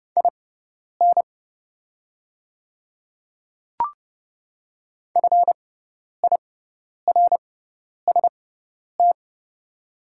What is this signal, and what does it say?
Morse code, heard as a keyed tone, about 31 wpm, spaced as for 10 wpm, 700 Hz. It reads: IN FIRST
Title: 6x Character Spacing - Top 500 Words - 30wpm